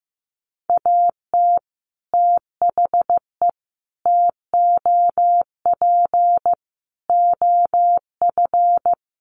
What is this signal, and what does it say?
Morse code, heard as a keyed tone, about 15 wpm, 700 Hz. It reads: AT THE TOP OF
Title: Sets of 5 Words Encore 15wpm